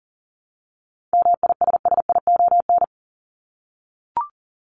Morse code, heard as a keyed tone, about 40 wpm, 700 Hz. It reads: MISSION
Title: Top 1501-1750 Words 40wpm